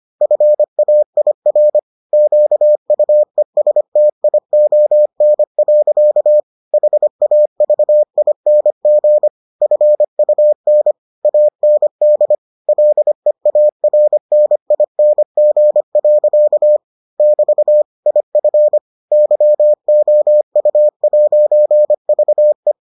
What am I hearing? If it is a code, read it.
Morse code, heard as a keyed tone, about 25 wpm, 600 Hz. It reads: FAIR QUESTION. HAVING FUN AND LEARNING. <BT> IF YOU'VE